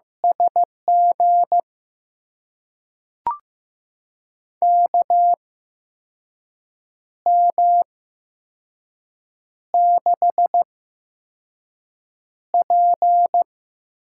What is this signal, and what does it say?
Morse code, heard as a keyed tone, about 15 wpm, spaced as for 5 wpm, 700 Hz. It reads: SG KM6P
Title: USA Call Signs - 6 Characters - 8x Character Spacing - 15wpm